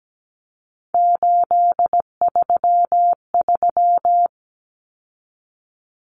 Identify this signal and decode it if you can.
Morse code, heard as a keyed tone, about 17 wpm, 690 Hz. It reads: 833
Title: Common Cut Numbers - 3-Digit 17wpm